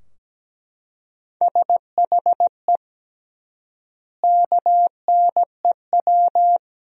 Morse code, heard as a keyed tone, about 17 wpm, 700 Hz. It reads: SHE KNEW
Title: 3x Word-Spacing - Sentences from Top 400 Words - 17wpm